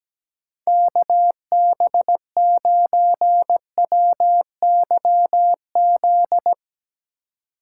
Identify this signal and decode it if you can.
Morse code, heard as a keyed tone, about 17 wpm, 700 Hz. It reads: KB9WYZ
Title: USA Call Signs - 6 Characters - 4x Character Spacing - 17wpm